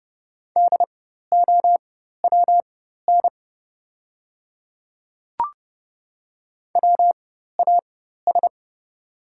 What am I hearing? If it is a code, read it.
Morse code, heard as a keyed tone, about 30 wpm, spaced as for 14 wpm, 700 Hz. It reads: DOWN WAS
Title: 4x Character Spacing - Top 100 Words - 30wpm